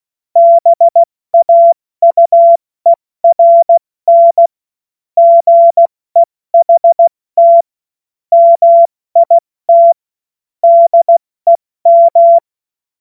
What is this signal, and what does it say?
Morse code, heard as a keyed tone, about 16 wpm, 680 Hz. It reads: BAUERN GEHT MIT DEM
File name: Witz-29-Juli-2025-70-F80